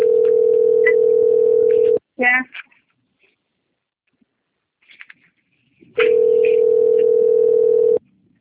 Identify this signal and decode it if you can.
Ringback tone (North America), 440 + 480 Hz.
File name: out- - - - - .31569